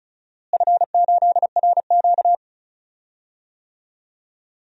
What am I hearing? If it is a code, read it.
Morse code, heard as a keyed tone, about 34 wpm, 700 Hz. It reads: F8RQ
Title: DXCC-100 Call Signs - 4 Characters - Repeated 3x - 35wpm